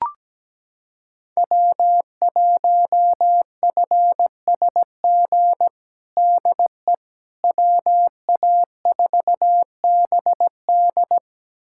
Morse code, heard as a keyed tone, about 17 wpm, 700 Hz. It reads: W1FSG DE WA4BD
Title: USA Callsign DE Callsign - 5 Characters - 17wpm